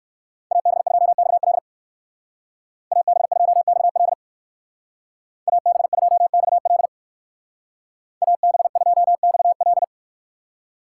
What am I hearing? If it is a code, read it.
Morse code, heard as a keyed tone, about 59 wpm, 700 Hz. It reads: AB2XL AB2XL AB2XL AB2XL